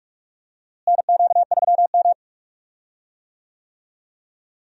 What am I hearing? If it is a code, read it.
Morse code, heard as a keyed tone, about 45 wpm, 700 Hz. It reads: NQ3M
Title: USA Call Signs - 4 Characters - Repeated 2x - 45wpm